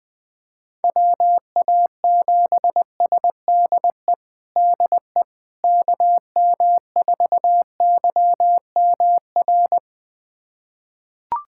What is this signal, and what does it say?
Morse code, heard as a keyed tone, about 21 wpm, 700 Hz. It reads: WA7SDE DE KM4YMR
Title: USA Callsign DE Callsign - 6 Characters - 20wpm